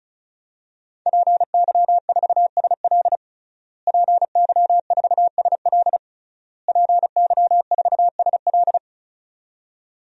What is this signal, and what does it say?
Morse code, heard as a keyed tone, about 36 wpm, 700 Hz. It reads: PY4SL PY4SL PY4SL
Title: DXCC-100 Call Signs - 5 Characters - Repeated 3x - 35wpm